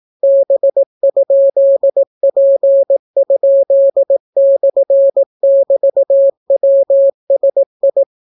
Morse code, heard as a keyed tone, about 18 wpm, 550 Hz. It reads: B?P?/<BT>WSI